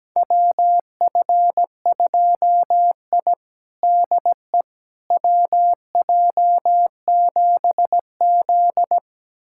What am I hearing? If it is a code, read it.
Morse code, heard as a keyed tone, about 17 wpm, 700 Hz. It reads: WF2I DE WJ7Z
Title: USA Callsign DE Callsign - 4 Characters - 17wpm